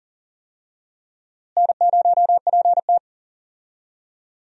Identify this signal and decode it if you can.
Morse code, heard as a keyed tone, about 40 wpm, 700 Hz. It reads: N0PT
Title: USA Call Signs - 4 Characters - Repeated 3x - 40wpm